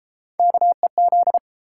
Morse code, heard as a keyed tone, about 29 wpm, 700 Hz. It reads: KEZ